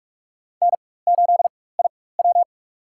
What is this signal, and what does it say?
Morse code, heard as a keyed tone, about 47 wpm, spaced as for 21 wpm, 700 Hz. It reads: N8IW
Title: USA Call Signs - 4 Characters - 4x Character Spacing - 45wpm